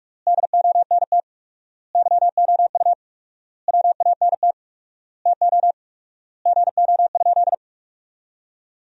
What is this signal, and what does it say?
Morse code, heard as a keyed tone, about 41 wpm, 700 Hz. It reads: DONT YOU WANT TO GO?
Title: 4x Word-Spacing - Sentences from Top 300 Words - 45wpm